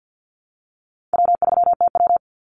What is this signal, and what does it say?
Morse code, heard as a keyed tone, about 50 wpm, 700 Hz. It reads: W3TW
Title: Easy Call Signs - 50wpm